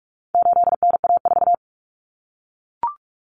Morse code, heard as a keyed tone, about 44 wpm, 700 Hz. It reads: 8NA4
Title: Common Cut Numbers - 4-Digit 45wpm